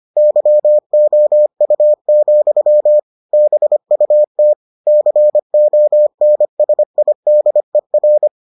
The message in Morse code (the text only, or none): YOU, BUT CONSIDER